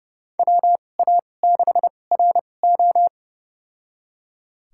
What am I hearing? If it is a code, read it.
Morse code, heard as a keyed tone, about 30 wpm, spaced as for 22 wpm, 700 Hz. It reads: WA6RO
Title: USA Call Signs - 5 Characters - 2x Character Spacing - 30wpm